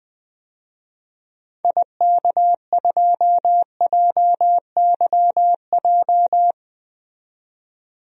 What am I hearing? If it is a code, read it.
Morse code, heard as a keyed tone, about 20 wpm, 700 Hz. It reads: IK2JYJ
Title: DXCC-100 Call Signs - 6 Characters - 20wpm